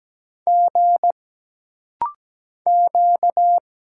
Morse code, heard as a keyed tone, about 17 wpm, 700 Hz. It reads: G Q
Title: Single Letters - Rapid-Fire 17wpm